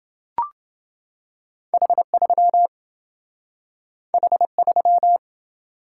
Morse code, heard as a keyed tone, about 27 wpm, 700 Hz. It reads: H3 H3